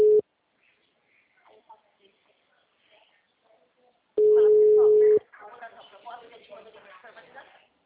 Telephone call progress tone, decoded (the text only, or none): ringback tone